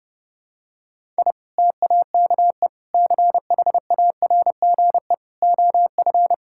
Morse code, heard as a keyed tone, about 30 wpm, 700 Hz. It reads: I TAKE CHARGE OF
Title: Sentences from Top 1000 Words - Review Part 6 - 30wpm